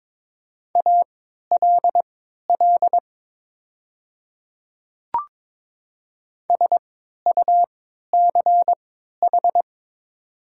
Morse code, heard as a keyed tone, about 22 wpm, spaced as for 13 wpm, 700 Hz. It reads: ALL SUCH
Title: 3x Character Spacing - Top 200 Words - 22wpm